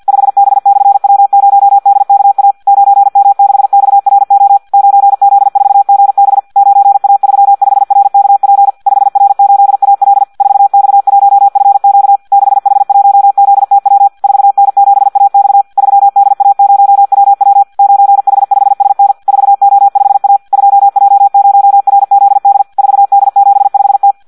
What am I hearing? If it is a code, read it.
Morse code, heard as a keyed tone, about 50 wpm, 800 Hz. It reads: 5B/W0DKA 9M6XRO 9Z4GB 9A35RKP 5R8AL 4X1FQ 6H1ZTW 4N7AX 3DA0WW 9H5SN 4O5A 2J0RZD 4D75T
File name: Audio practica CW / qrq-tx-50 2